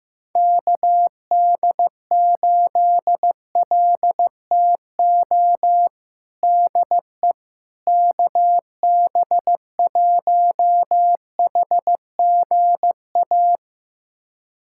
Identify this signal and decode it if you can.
Morse code, heard as a keyed tone, about 16 wpm, 700 Hz. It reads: KD8LTO DE KB1HGA
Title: USA Callsign DE Callsign - 6 Characters - 15wpm